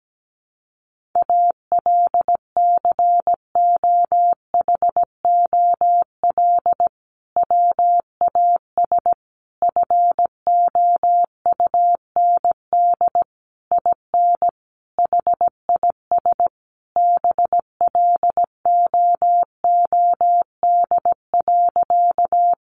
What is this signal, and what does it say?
Morse code, heard as a keyed tone, about 17 wpm, 700 Hz. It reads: ALCOHOL WAS FOUND IN HIS BLOOD.
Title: Sentences from Top 1400 Words 17wpm